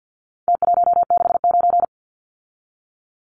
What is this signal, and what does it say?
Morse code, heard as a keyed tone, about 48 wpm, 700 Hz. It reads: T169
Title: Numbers - Contest Style - 4-Digit 50wpm